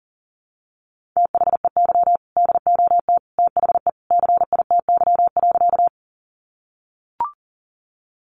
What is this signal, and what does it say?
Morse code, heard as a keyed tone, about 40 wpm, 700 Hz. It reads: THEY DOT THE CITY.
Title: Sentences from Top 1500 Words 40wpm